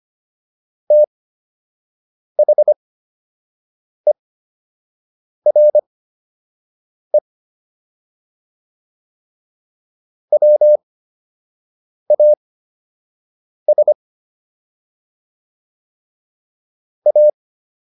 Morse code, heard as a keyed tone, about 25 wpm, spaced as for 6 wpm, 600 Hz. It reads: THERE WAS A